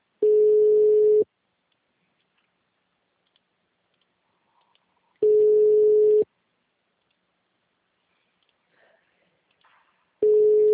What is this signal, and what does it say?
Ringback tone (Europe, 425 Hz).
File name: out- -2002- - - .55812